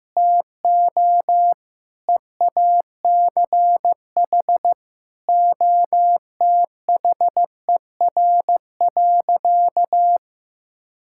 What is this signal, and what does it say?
Morse code, heard as a keyed tone, about 15 wpm, 700 Hz. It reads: TO EACH OTHER.
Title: Sentences from Top 1000 Words - Review Part 5 - 15wpm